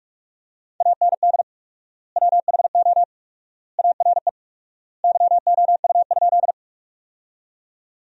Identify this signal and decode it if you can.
Morse code, heard as a keyed tone, about 48 wpm, 700 Hz. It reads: AND WHO ARE YOU?
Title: 4x Word-Spacing - Sentences from Top 100 Words - 45wpm